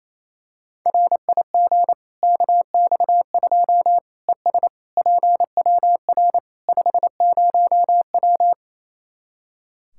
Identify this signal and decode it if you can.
Morse code, heard as a keyed tone, about 28 wpm, 700 Hz. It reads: RIG KX2 ES PWR 50W
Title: Practice QSO - Exchange 2 - Info - 28wpm